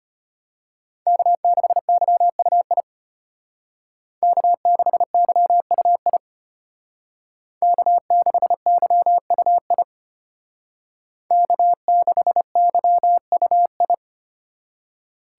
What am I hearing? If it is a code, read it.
Morse code, heard as a keyed tone, about 32 wpm, 700 Hz. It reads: K6YUI K6YUI K6YUI K6YUI